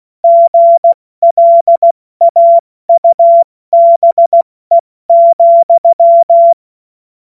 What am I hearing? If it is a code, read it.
Morse code, heard as a keyed tone, about 16 wpm, 680 Hz. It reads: GLAUBE,
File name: Morsekurs / Witze / Witz 08. November 2022 70 F80